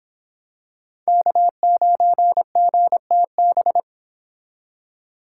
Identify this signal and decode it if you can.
Morse code, heard as a keyed tone, about 26 wpm, 700 Hz. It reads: K9GTB